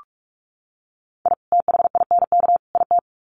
Morse code, heard as a keyed tone, about 44 wpm, 700 Hz. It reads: I THINK IT
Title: Sets of 3 Words 45wpm